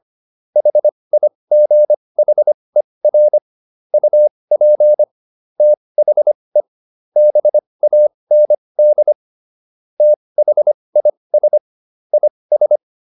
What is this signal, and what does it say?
Morse code, heard as a keyed tone, about 25 wpm, spaced as for 20 wpm, 600 Hz. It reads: HIGHER UP THE BAND THIS IS